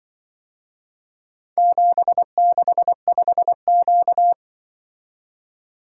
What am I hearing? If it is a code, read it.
Morse code, heard as a keyed tone, about 24 wpm, 700 Hz. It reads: 765Q
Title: Speed-Racing - Four Letter-Number Combinations - 20wpm